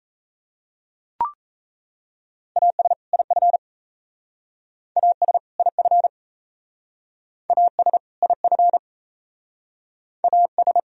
Morse code, heard as a keyed tone, about 36 wpm, 700 Hz. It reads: AS IF AS IF AS IF AS